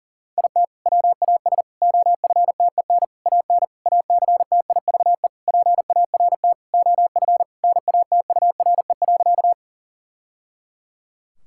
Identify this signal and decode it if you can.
Morse code, heard as a keyed tone, about 40 wpm, 700 Hz. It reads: IT WAS OFTEN AN ACTIVE PART OF NATURE.
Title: Sentences from Top 1000 Words - Review Part 6 - 40wpm